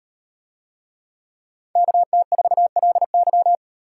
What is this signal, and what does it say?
Morse code, heard as a keyed tone, about 38 wpm, 700 Hz. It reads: KT4LY